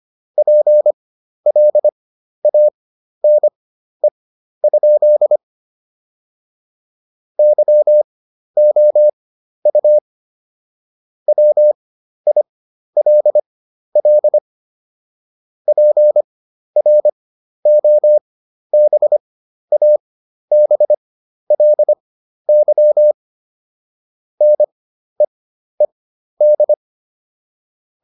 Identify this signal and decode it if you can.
Morse code, heard as a keyed tone, about 25 wpm, spaced as for 12 wpm, 600 Hz. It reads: PLANE? YOU WILL PROBABLY NEED